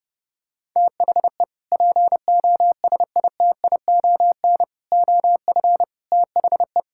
Morse code, heard as a keyed tone, about 30 wpm, 700 Hz. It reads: THE POSITION OF THE
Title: Sentences from Top 1600 Words 30wpm